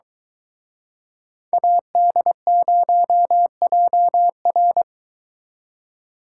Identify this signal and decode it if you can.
Morse code, heard as a keyed tone, about 23 wpm, 700 Hz. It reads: AD0JR